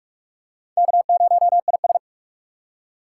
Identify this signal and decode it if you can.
Morse code, heard as a keyed tone, about 44 wpm, 700 Hz. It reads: K0IS
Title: USA Call Signs - 4 Characters - 4x Character Spacing - 45wpm